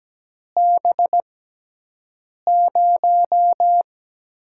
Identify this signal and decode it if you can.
Morse code, heard as a keyed tone, about 17 wpm, spaced as for 6 wpm, 700 Hz. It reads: B0
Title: USA Call Signs - 5 Characters - 6x Character Spacing - 17wpm